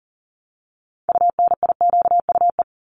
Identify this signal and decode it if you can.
Morse code, heard as a keyed tone, about 42 wpm, 700 Hz. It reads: UNIQUE
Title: Multi-letter Phonemes 40wpm